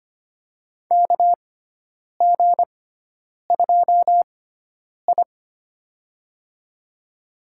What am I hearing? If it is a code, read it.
Morse code, heard as a keyed tone, about 25 wpm, spaced as for 9 wpm, 700 Hz. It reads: KG2I
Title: USA Call Signs - 4 Characters - 6x Character Spacing - 25wpm